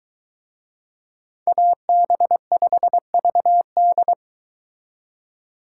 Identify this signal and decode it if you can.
Morse code, heard as a keyed tone, about 24 wpm, 700 Hz. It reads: AB5VD